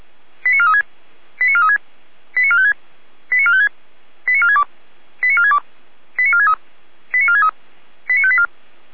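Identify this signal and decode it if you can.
Five-tone selective call (ZVEI), heard as quick runs of five tones, tones of about 70 ms. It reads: 89436 89436 89456 89456 89462 89462 89463 89463 89584
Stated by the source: ZVEI five-tone selective call 89417 received by SDR.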